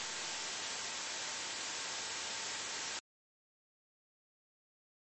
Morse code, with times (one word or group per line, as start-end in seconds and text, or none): none